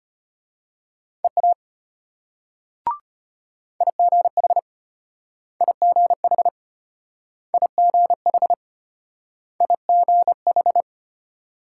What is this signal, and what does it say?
Morse code, heard as a keyed tone, about 34 wpm, 700 Hz. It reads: EA IGH IGH IGH IGH